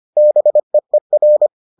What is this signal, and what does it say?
Morse code, heard as a keyed tone, about 25 wpm, 600 Hz. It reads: BEER